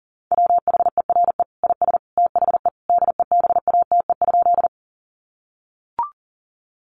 Morse code, heard as a keyed tone, about 40 wpm, 700 Hz. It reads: WHERE IS THE DEBATE?
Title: Sentences from Top 1100 Words 40wpm